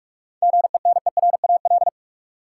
Morse code, heard as a keyed tone, about 46 wpm, 700 Hz. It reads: GENERAL
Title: Sentences from Top 600 Words - New Word Drill - 45wpm